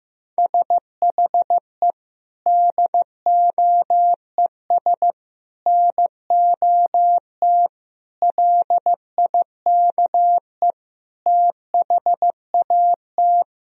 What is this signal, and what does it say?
Morse code, heard as a keyed tone, about 16 wpm, 700 Hz. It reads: SHE DOES NOT LIKE THAT
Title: Sentences from Top 1000 Words - Review Part 4 - 15wpm